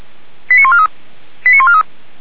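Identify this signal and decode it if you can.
Five-tone selective call (ZVEI), heard as quick runs of five tones, tones of about 70 ms. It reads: 89143 89143
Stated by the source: Five-tone ZVEI selective call 89180 received by SDR.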